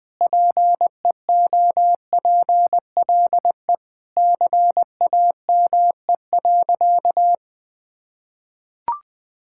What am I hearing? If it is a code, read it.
Morse code, heard as a keyed tone, about 20 wpm, 700 Hz. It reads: PEOPLE CAME.
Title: Sentences with 3-Word Phrases - 20wpm